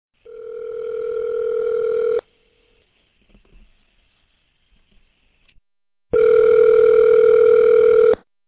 Ringback tone (North America), 440 + 480 Hz.